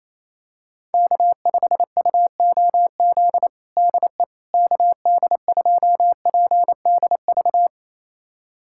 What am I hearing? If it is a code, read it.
Morse code, heard as a keyed tone, about 28 wpm, 700 Hz. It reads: K5UOZ DE KD2PDV
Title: USA Callsign DE Callsign 28wpm